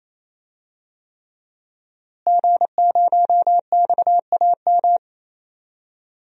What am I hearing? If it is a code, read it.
Morse code, heard as a keyed tone, about 28 wpm, 700 Hz. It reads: G0XAM
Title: DXCC-100 Call Signs - 5 Characters - Repeated 3x - 28wpm